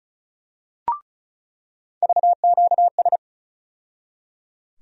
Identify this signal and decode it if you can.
Morse code, heard as a keyed tone, about 34 wpm, 700 Hz. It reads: VQS
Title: USA Call Sign Postfixes - 3 Letters 35wpm